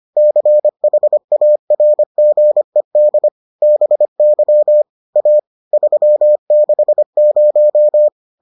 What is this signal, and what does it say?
Morse code, heard as a keyed tone, about 25 wpm, 600 Hz. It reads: CHARGED BY A 360